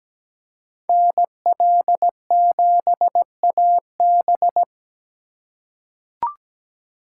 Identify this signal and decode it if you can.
Morse code, heard as a keyed tone, about 17 wpm, 700 Hz. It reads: NL7AB
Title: USA Call Signs - 3x Character Spacing - 17wpm